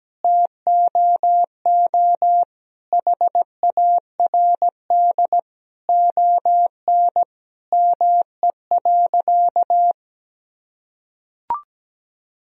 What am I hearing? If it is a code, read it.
Morse code, heard as a keyed tone, about 17 wpm, 700 Hz. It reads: TOO HARD ON ME.
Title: Sentences from Top 1000 Words - Review Part 1 - 17wpm